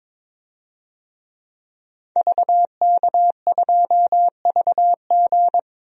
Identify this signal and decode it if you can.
Morse code, heard as a keyed tone, about 22 wpm, 700 Hz. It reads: VK2VG
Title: DXCC-100 Call Signs - 5 Characters - 22wpm